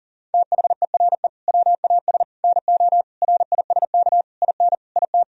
Morse code, heard as a keyed tone, about 40 wpm, 700 Hz. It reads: THERE WAS NO RISK IN IT
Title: Sentences from Top 1000 Words - Review Part 4 - 40wpm